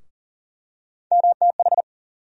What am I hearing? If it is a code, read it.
Morse code, heard as a keyed tone, about 39 wpm, 700 Hz. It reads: MTH
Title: USA Call Sign Postfixes - 3 Letters 40wpm